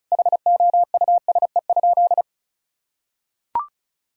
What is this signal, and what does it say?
Morse code, heard as a keyed tone, about 36 wpm, 700 Hz. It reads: HOUSE?
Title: Sentences from Top 1000 Words - Review Part 2 - 35wpm